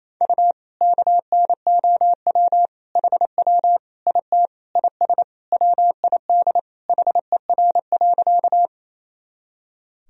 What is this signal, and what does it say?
Morse code, heard as a keyed tone, about 28 wpm, 700 Hz. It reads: U KNOW HW IT IS WID HER.
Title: Sentences from Top 100 Words and Abbreviations 28wpm